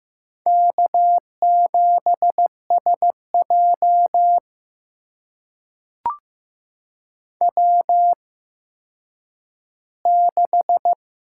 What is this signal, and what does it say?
Morse code, heard as a keyed tone, about 15 wpm, spaced as for 7 wpm, 700 Hz. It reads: K7SJW 6
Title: USA Call Signs - 8x Character Spacing - 15wpm